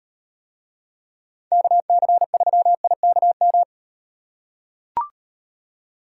Morse code, heard as a keyed tone, about 38 wpm, 700 Hz. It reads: KC3IKM